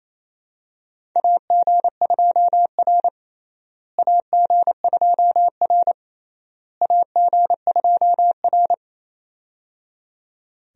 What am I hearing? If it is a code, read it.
Morse code, heard as a keyed tone, about 28 wpm, 700 Hz. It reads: AG2R AG2R AG2R
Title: USA Call Signs - 4 Characters - Repeated 3x - 28wpm